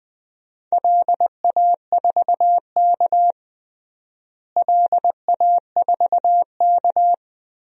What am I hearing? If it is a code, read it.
Morse code, heard as a keyed tone, about 20 wpm, 700 Hz. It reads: LA4K LA4K
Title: DXCC-100 Call Signs - 4 Characters - Repeated 2x - 20wpm